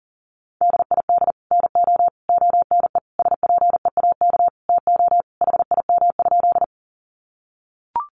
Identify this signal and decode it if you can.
Morse code, heard as a keyed tone, about 40 wpm, 700 Hz. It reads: DID NO ONE SPEAK TO HIM?
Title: Sentences from Top 1100 Words 40wpm